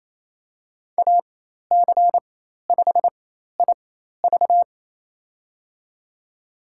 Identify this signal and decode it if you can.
Morse code, heard as a keyed tone, about 27 wpm, spaced as for 13 wpm, 700 Hz. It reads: AC5IV
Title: USA Call Signs - 5 Characters - 4x Character Spacing - 28wpm